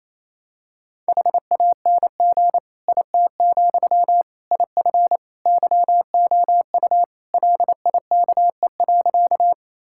Morse code, heard as a keyed tone, about 28 wpm, 700 Hz. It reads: HANG IT, IF YOU LIKE.
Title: Sentences from Top 1600 Words 28wpm